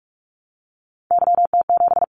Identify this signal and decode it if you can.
Morse code, heard as a keyed tone, about 46 wpm, 700 Hz. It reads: YT7
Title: Callsign Prefixes for Top 25 DXCC - 45wpm